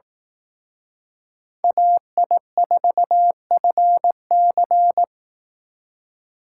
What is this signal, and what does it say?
Morse code, heard as a keyed tone, about 18 wpm, 700 Hz. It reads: AI4FC
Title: Speed Racing - USA 2x2 Call Signs 15wpm